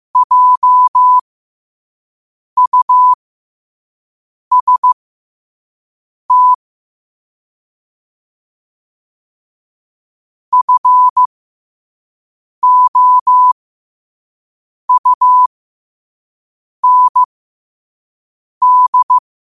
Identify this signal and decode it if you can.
Morse code, heard as a keyed tone, about 15 wpm, spaced as for 5 wpm, 1000 Hz. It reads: JUST FOUND
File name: 5wpm